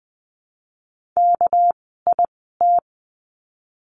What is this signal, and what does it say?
Morse code, heard as a keyed tone, about 20 wpm, spaced as for 15 wpm, 700 Hz. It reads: KIT
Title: 199 Lesson 32f - Letter k - Words 20wpm